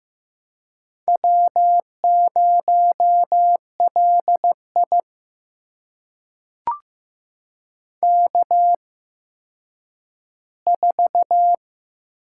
Morse code, heard as a keyed tone, about 15 wpm, spaced as for 9 wpm, 700 Hz. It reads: W0LI K4
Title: USA Call Signs - 4 Characters - 8x Character Spacing - 15wpm